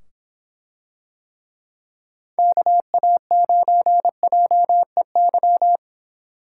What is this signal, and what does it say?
Morse code, heard as a keyed tone, about 25 wpm, 700 Hz. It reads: KA9JEY